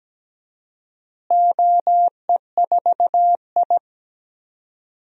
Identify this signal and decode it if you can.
Morse code, heard as a keyed tone, about 17 wpm, 700 Hz. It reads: OE4I
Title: DXCC-100 Call Signs - 4 Characters - Repeated 3x - 17wpm